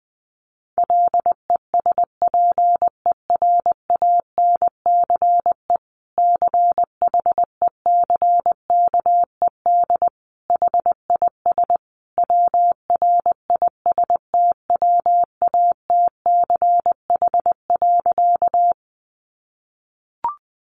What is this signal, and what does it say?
Morse code, heard as a keyed tone, about 20 wpm, 700 Hz. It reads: LESPERANCE CHECKED HIS WRISTWATCH.
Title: A Sound Of Thunder 20wpm